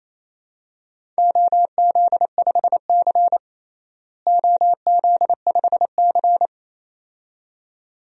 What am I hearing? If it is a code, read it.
Morse code, heard as a keyed tone, about 28 wpm, 700 Hz. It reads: OZ5C OZ5C
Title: DXCC-100 Call Signs - 4 Characters - Repeated 2x - 28wpm